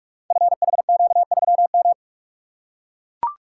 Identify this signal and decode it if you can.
Morse code, heard as a keyed tone, about 44 wpm, 700 Hz. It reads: FHQ3M
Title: Code Groups - 5 Letter:Numbers - 45wpm